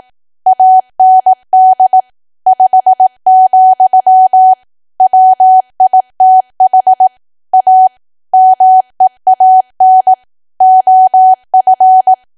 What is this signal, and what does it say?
Morse code, heard as a keyed tone, about 18 wpm, 750 Hz. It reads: AND 5, WITH A MEAN OF
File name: cwa / W1AW-Level2 / 18WPM